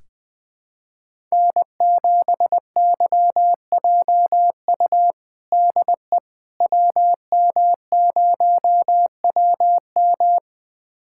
Morse code, heard as a keyed tone, about 21 wpm, 700 Hz. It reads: N7YJU DE WM0WM
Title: USA Callsign DE Callsign - 5 Characters - 20wpm